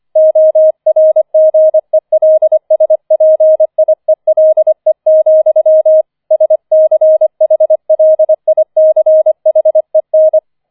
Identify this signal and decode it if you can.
Morse code, heard as a keyed tone, about 25 wpm, 630 Hz. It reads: ORGELSPIELE, SCHLICHEN
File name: 004 Vierter Streich 25wpm